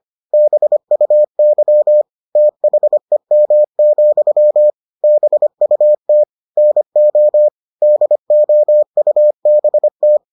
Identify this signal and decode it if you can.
Morse code, heard as a keyed tone, about 25 wpm, 600 Hz. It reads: BUY THEM, BUT NO DOUBT